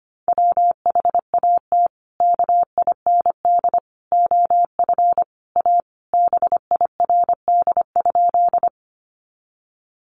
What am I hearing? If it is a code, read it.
Morse code, heard as a keyed tone, about 25 wpm, 700 Hz. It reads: WHAT KIND OF A BIRD?
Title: Sentences from Top 400 Words 25wpm